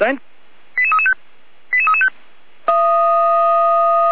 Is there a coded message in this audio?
Yes: five-tone selective call (ZVEI) from 0.8 to 2.1 s, heard as quick runs of five tones, tones of about 70 ms, with speech. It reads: 88335 88335